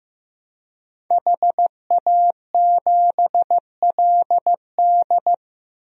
Morse code, heard as a keyed tone, about 15 wpm, 700 Hz. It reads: HA7LD
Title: DXCC-100 Call Signs - 5 Characters - Repeated 3x - 15wpm